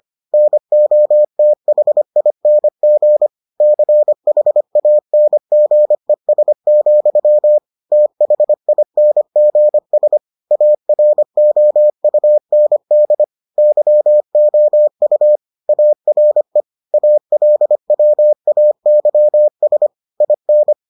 Morse code, heard as a keyed tone, about 25 wpm, 600 Hz. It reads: NOTHING CHANGES, THINGS AROUND YOU ARE ALWAYS IN